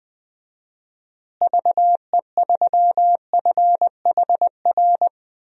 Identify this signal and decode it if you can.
Morse code, heard as a keyed tone, about 21 wpm, 700 Hz. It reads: VE3FHR
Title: DXCC-100 Call Signs - 6 Characters - Repeated 2x - 20wpm